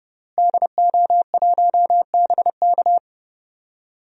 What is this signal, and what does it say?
Morse code, heard as a keyed tone, about 30 wpm, 700 Hz. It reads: DO1BK
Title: DXCC-100 Call Signs - 5 Characters - Repeated 3x - 30wpm